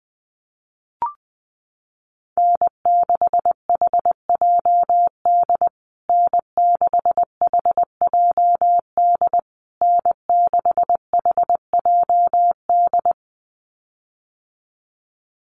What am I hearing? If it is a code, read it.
Morse code, heard as a keyed tone, about 20 wpm, 700 Hz. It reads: N6HJD N6HJD N6HJD
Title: USA Call Signs Repeated 3x 20wpm